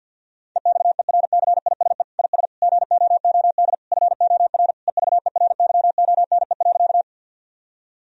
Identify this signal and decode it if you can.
Morse code, heard as a keyed tone, about 52 wpm, 700 Hz. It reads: EXERCISE IS GOOD FOR EVERYONE.
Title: Sentences from Top 1000 Words - Review Part 4 - 50wpm